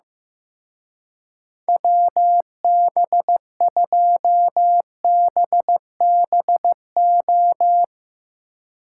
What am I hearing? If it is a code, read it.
Morse code, heard as a keyed tone, about 15 wpm, 700 Hz. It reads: WB2BBO
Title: Speed Racing - USA 2x3 Call Signs 15wpm